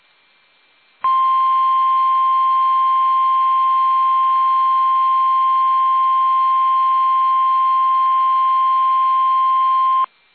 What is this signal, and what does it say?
Emergency Alert System (SAME) tones (the 1050 Hz attention tone). It is heard from 1.0 to 10.1 s.
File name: eas / KUSO / KUSO